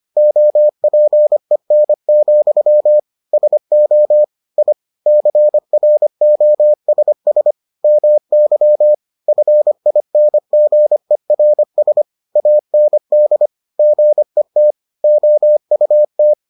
Morse code, heard as a keyed tone, about 25 wpm, 600 Hz. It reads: OPEN, SO I CROSS MY FINGERS AND GET OUT